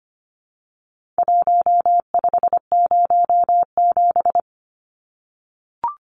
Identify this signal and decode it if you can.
Morse code, heard as a keyed tone, about 25 wpm, 700 Hz. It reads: 1507
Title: Numbers - Contest Style - 4-Digit 25wpm